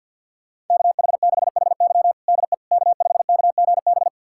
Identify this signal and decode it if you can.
Morse code, heard as a keyed tone, about 51 wpm, 700 Hz. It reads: KH6HY DE K5KGB
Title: USA Callsign DE Callsign - 5 Characters - 50wpm